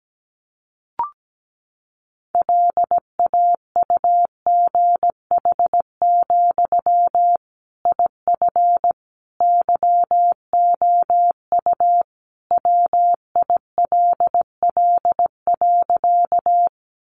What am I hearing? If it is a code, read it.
Morse code, heard as a keyed tone, about 17 wpm, 700 Hz. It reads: LAUGH, IF YOU WILL.
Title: Sentences from Top 500 Words 17wpm